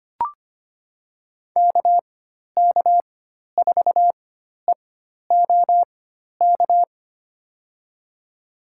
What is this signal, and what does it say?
Morse code, heard as a keyed tone, about 25 wpm, spaced as for 12 wpm, 700 Hz. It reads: KK4EOK
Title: USA Call Signs - 6 Characters - 4x Character Spacing - 25wpm